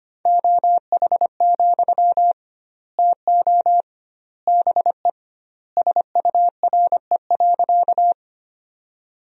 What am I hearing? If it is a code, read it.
Morse code, heard as a keyed tone, about 25 wpm, 700 Hz. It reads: OH, TO BE SURE.
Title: 2x Word-Spacing - Sentences from Top 500 Words - 25wpm